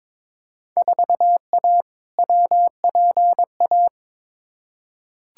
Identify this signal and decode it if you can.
Morse code, heard as a keyed tone, about 22 wpm, 700 Hz. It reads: 4A WPA
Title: Field Day Exchanges Encore - 22wpm